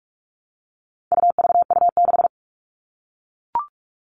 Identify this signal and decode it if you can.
Morse code, heard as a keyed tone, about 44 wpm, 700 Hz. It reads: UVU6
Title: All Cut Numbers - 4-Digit 45wpm